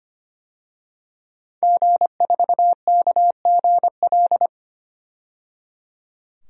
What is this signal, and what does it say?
Morse code, heard as a keyed tone, about 25 wpm, 700 Hz. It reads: G4KGL
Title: DXCC-100 Call Signs - 5 Characters - 25wpm